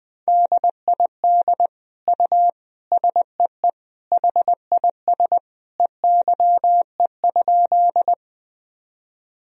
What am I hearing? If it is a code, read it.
Morse code, heard as a keyed tone, about 21 wpm, 700 Hz. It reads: DID U SEE HIS EYE?
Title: Sentences from Top 300 Words and Abbreviations 20wpm